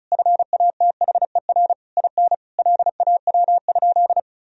Morse code, heard as a keyed tone, about 34 wpm, 700 Hz. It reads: FATHER IN LAW?
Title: Sentences from Top 1000 Words - Review Part 4 - 35wpm